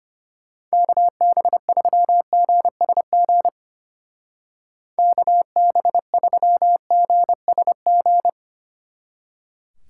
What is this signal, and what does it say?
Morse code, heard as a keyed tone, about 32 wpm, 700 Hz. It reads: KB3GSG KB3GSG